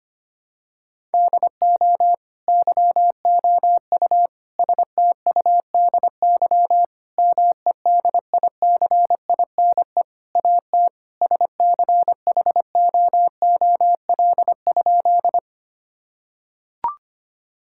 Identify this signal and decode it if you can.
Morse code, heard as a keyed tone, about 25 wpm, 700 Hz. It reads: DO YOU STUDY MEDICINE AT SCHOOL?
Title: Sentences from Top 1000 Words - Review Part 6 - 25wpm